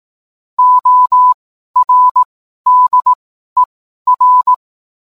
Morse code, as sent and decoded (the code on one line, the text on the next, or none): --- .-. -.. .-.
ORDER